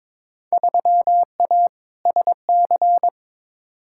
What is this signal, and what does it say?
Morse code, heard as a keyed tone, about 23 wpm, 700 Hz. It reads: 3A SC
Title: Field Day Exchanges - 22wpm